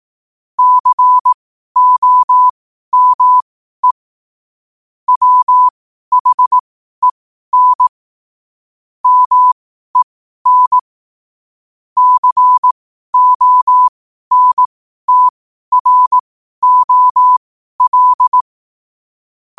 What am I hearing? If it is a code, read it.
Morse code, heard as a keyed tone, about 18 wpm, spaced as for 12 wpm, 1000 Hz. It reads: COME WHEN MEN CONTROL
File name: qotdfiles / 12wpm